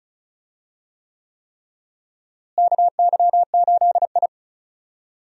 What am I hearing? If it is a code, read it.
Morse code, heard as a keyed tone, about 36 wpm, 700 Hz. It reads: KY8I
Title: USA Call Signs - 4 Characters - Repeated 3x - 35wpm